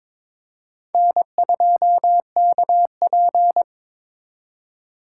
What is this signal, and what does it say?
Morse code, heard as a keyed tone, about 26 wpm, 700 Hz. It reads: N2KP
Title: Speed Racing - USA 1x2 Call Signs 22wpm